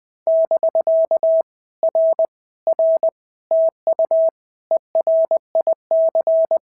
Morse code, heard as a keyed tone, about 20 wpm, 650 Hz. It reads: <BK> R R TU ERIC
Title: POTA QSO Practice - 20wpm